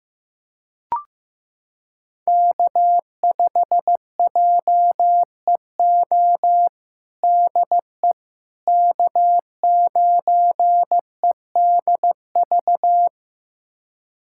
Morse code, heard as a keyed tone, about 16 wpm, 700 Hz. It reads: K5JEO DE K9EDV
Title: USA Callsign DE Callsign - 5 Characters - 15wpm